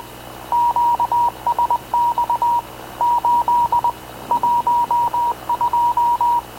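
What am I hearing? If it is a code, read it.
Morse code, heard as a keyed tone, about 20 wpm, 960 Hz. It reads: QSX 8 12
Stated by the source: Havana Radio / CLA sending Spanish press (PX) in Morse, recorded by Radio Officer Paul Dumesnil onboard CNAV QUEST/CZDO.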